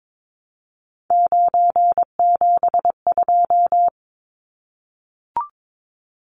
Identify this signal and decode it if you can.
Morse code, heard as a keyed tone, about 22 wpm, 700 Hz. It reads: 972
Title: Numbers - 3-Digit 22wpm